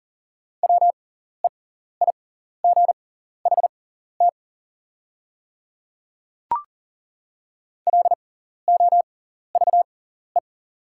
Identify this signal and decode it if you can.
Morse code, heard as a keyed tone, about 40 wpm, spaced as for 14 wpm, 690 Hz. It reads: WEIGHT LOVE
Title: 6x Character Spacing - Top 500 Words - 40wpm